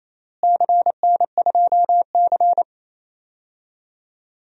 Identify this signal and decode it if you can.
Morse code, heard as a keyed tone, about 27 wpm, 700 Hz. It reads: CN2C
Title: DXCC-100 Call Signs - 4 Characters - Repeated 3x - 28wpm